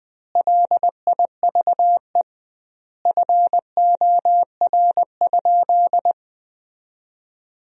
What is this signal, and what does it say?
Morse code, heard as a keyed tone, about 20 wpm, 700 Hz. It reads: LIVE FOR?
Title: 2x Word-Spacing - Sentences from Top 200 Words - 20wpm